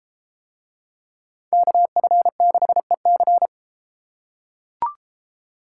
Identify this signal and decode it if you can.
Morse code, heard as a keyed tone, about 31 wpm, 700 Hz. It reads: KF6EC